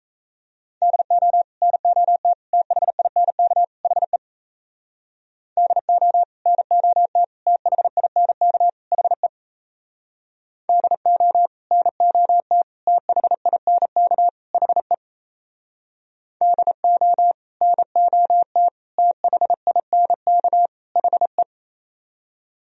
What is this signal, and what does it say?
Morse code, heard as a keyed tone, about 37 wpm, 700 Hz. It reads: DO NOT THINK HE DO NOT THINK HE DO NOT THINK HE DO NOT THINK HE